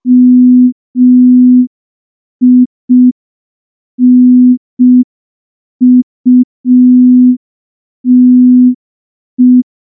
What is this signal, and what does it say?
Morse code, heard as a keyed tone, about 5 wpm, 250 Hz. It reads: MINUTE
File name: morse / morse-training-05WPM